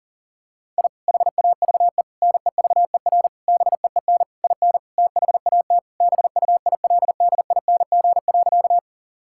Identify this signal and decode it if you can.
Morse code, heard as a keyed tone, about 40 wpm, 700 Hz. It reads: I HAVE NEVER BEEN IN THAT BUILDING.
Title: Sentences from Top 1000 Words - Review Part 5 - 40wpm